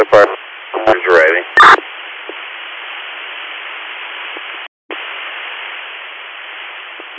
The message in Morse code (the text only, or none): none